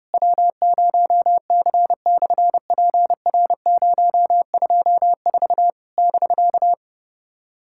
Morse code, heard as a keyed tone, about 30 wpm, 700 Hz. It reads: W0C/PR024 <BK>
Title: SOTA QSO Practice 30wpm